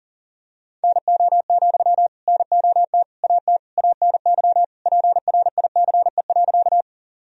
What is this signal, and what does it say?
Morse code, heard as a keyed tone, about 40 wpm, 700 Hz. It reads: NO, NOT AT ANY PRICE.
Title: Sentences from Top 1000 Words - Review Part 4 - 40wpm